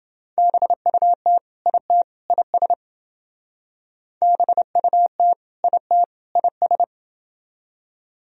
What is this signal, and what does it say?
Morse code, heard as a keyed tone, about 29 wpm, 700 Hz. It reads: BUT IT IS BUT IT IS